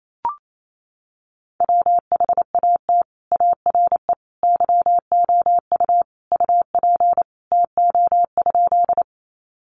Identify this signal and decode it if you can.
Morse code, heard as a keyed tone, about 28 wpm, 700 Hz. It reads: WHAT ARE YOU UP TO?
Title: Sentences from Top 100 Words 28wpm